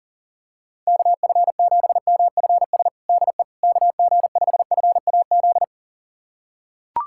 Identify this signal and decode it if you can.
Morse code, heard as a keyed tone, about 40 wpm, 700 Hz. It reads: KF7MFS DE KG5FAZ
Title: USA Callsign DE Callsign - 6 Characters - 40wpm